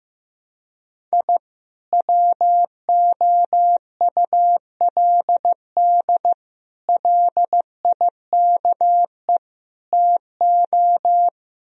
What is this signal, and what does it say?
Morse code, heard as a keyed tone, about 15 wpm, 700 Hz. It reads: I WOULD LIKE TO
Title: Speed-Racing - Sets of 4 Words 15wpm